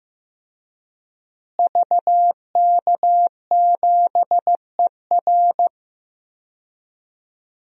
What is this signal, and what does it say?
Morse code, heard as a keyed tone, about 15 wpm, 700 Hz. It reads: VK7ER
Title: DXCC-100 Call Signs - 5 Characters - Repeated 2x - 15wpm